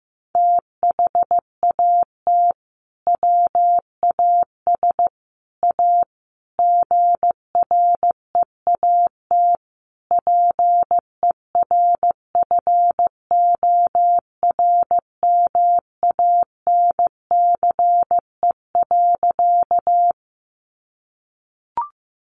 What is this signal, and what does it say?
Morse code, heard as a keyed tone, about 15 wpm, 700 Hz. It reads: THAT WAS A GREAT PERFORMANCE.
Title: Sentences from Top 1000 Words 15wpm